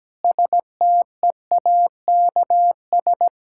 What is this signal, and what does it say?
Morse code, heard as a keyed tone, about 17 wpm, 700 Hz. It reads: STEAKS
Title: Pangram Challenge v2 - 17wpm